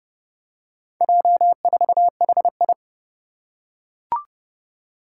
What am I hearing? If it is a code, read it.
Morse code, heard as a keyed tone, about 30 wpm, 700 Hz. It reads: J4HI
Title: Four Challenging Letter-Number Combinations 30wpm